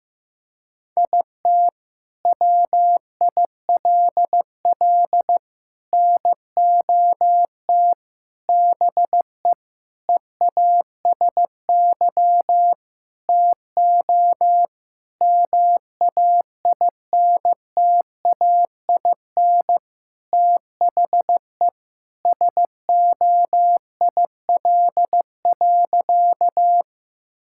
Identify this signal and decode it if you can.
Morse code, heard as a keyed tone, about 15 wpm, 700 Hz. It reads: IT WILL NOT BE EASY TO MAINTAIN THE SOIL.
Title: Sentences from Top 1000 Words - Review Part 5 - 15wpm